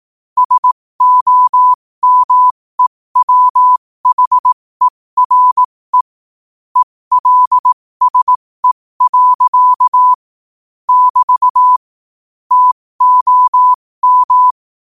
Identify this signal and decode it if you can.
Morse code, heard as a keyed tone, about 18 wpm, spaced as for 15 wpm, 1000 Hz. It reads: SOMEWHERE ELSE. <BT> TOM
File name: qotdfiles / 15wpm